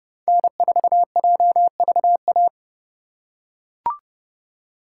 Morse code, heard as a keyed tone, about 32 wpm, 700 Hz. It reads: N4JVA
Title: USA Call Signs - 5 Characters - 3x Character Spacing - 30wpm